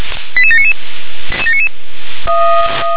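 Five-tone selective call (ZVEI), heard as quick runs of five tones, tones of about 70 ms. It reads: 88799
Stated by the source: ZVEI five-tone selective call for code 88795, received by SDR.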